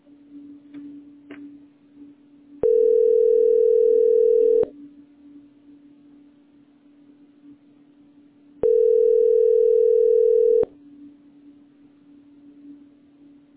Ringback tone (North America), 440 + 480 Hz.